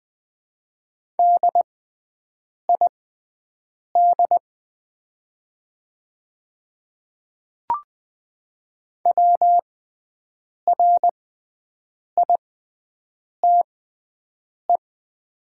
Morse code, heard as a keyed tone, about 20 wpm, spaced as for 7 wpm, 700 Hz. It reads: DID WRITE
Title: 6x Character Spacing - Top 200 Words - 20wpm